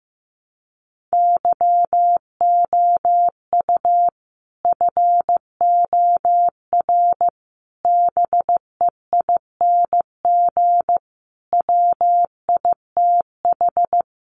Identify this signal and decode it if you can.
Morse code, heard as a keyed tone, about 15 wpm, 700 Hz. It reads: YOU FOR BEING WITH
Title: Sets of 5 Words 15wpm